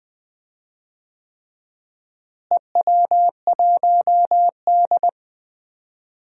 Morse code, heard as a keyed tone, about 20 wpm, 700 Hz. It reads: EW1D
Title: DXCC-100 Call Signs - 4 Characters - Repeated 3x - 20wpm